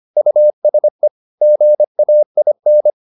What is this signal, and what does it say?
Morse code, heard as a keyed tone, about 25 wpm, 600 Hz. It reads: USE GAIN